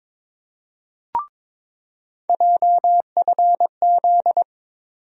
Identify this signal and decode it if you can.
Morse code, heard as a keyed tone, about 22 wpm, 700 Hz. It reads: JFZ
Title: USA Call Sign Postfixes - 3 Letters 22wpm